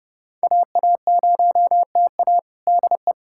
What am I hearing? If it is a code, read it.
Morse code, heard as a keyed tone, about 32 wpm, 700 Hz. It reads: AA0TA DE
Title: USA Callsign DE Callsign - 5 Characters - 30wpm